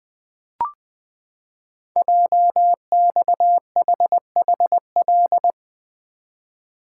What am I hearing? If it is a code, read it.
Morse code, heard as a keyed tone, about 20 wpm, 700 Hz. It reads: JXHHL
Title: Code Groups - 5 Letters - 20wpm